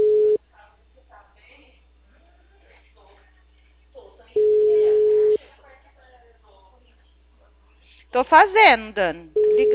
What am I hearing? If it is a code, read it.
Ringback tone (Europe, 425 Hz), with speech.
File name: out- -2001- - - .84